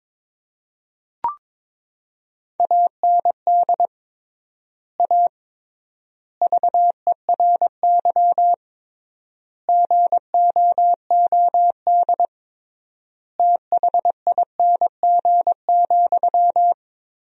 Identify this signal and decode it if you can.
Morse code, heard as a keyed tone, about 22 wpm, 700 Hz. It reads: AND A VERY GOOD THING,
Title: 3x Word-Spacing - Sentences from Top 200 Words - 22wpm